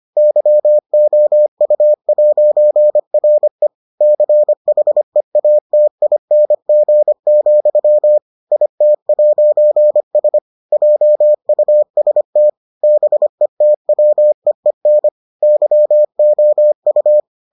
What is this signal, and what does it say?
Morse code, heard as a keyed tone, about 25 wpm, 600 Hz. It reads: YOU'RE CHEATING, IT'S JUST BETWEEN YOU